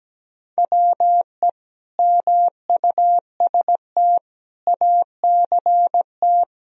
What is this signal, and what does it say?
Morse code, heard as a keyed tone, about 17 wpm, 700 Hz. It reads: WE MUST ACT
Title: Sentences from Top 1000 Words - Review Part 1 - 17wpm